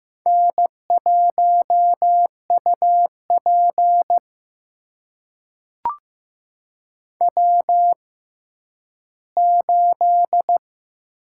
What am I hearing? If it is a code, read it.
Morse code, heard as a keyed tone, about 15 wpm, spaced as for 10 wpm, 700 Hz. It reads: N1UP W8
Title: USA Call Signs - 4 Characters - 6x Character Spacing - 15wpm